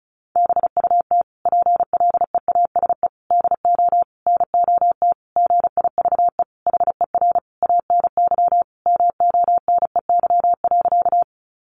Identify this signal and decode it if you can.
Morse code, heard as a keyed tone, about 34 wpm, 700 Hz. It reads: BUT PLEASE DO NOT GIVE HER ANY MONEY.
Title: Sentences from Top 1500 Words 35wpm